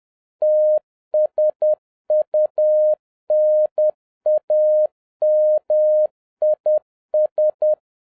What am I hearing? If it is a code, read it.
Morse code, heard as a keyed tone, about 10 wpm, 610 Hz. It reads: TSUNAMIS
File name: 2011-07-mp3 / n31 THE 3.11 2 10wpm